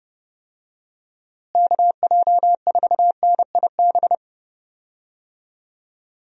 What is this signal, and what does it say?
Morse code, heard as a keyed tone, about 32 wpm, 700 Hz. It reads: KJ4NIB
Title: USA Call Signs - 6 Characters - Repeated 3x - 30wpm